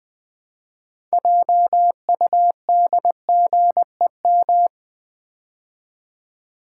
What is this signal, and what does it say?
Morse code, heard as a keyed tone, about 20 wpm, 700 Hz. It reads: JUDGEM
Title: Build A Word - 3 Letters At A Time - 9-Letter Words - 20wpm